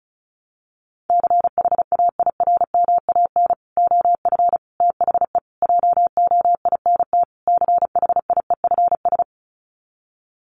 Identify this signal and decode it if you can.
Morse code, heard as a keyed tone, about 34 wpm, 700 Hz. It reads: CHAIRMAN OF THE JOINT CHIEFS
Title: Sets of 5 Words Encore 35wpm